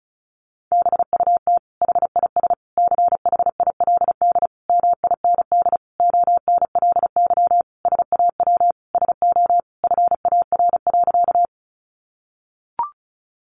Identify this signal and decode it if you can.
Morse code, heard as a keyed tone, about 34 wpm, 700 Hz. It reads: BUT HIS CHILD MIND ONLY SAW SO FAR.
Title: Sentences from Top 800 Words 35wpm